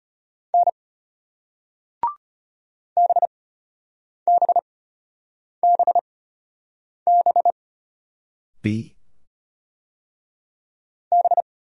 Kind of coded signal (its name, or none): Morse code